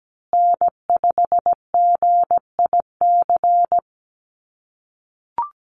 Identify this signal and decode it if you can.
Morse code, heard as a keyed tone, about 17 wpm, 700 Hz. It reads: N5GIC
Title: USA Call Signs Repeated 2x 17wpm